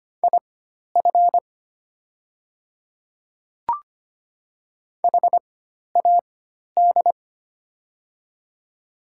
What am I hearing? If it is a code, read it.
Morse code, heard as a keyed tone, about 25 wpm, spaced as for 12 wpm, 690 Hz. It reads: IF HAD
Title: 4x Character Spacing - Top 100 Words - 25wpm